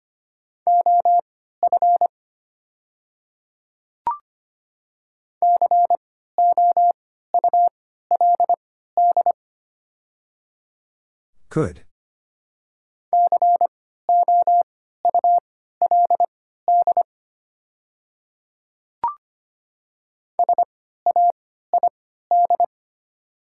Morse code, heard as a keyed tone, about 25 wpm, spaced as for 14 wpm, 700 Hz. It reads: OF COULD COULD SAID
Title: 3x Character Spacing - Top 100 Words - 25wpm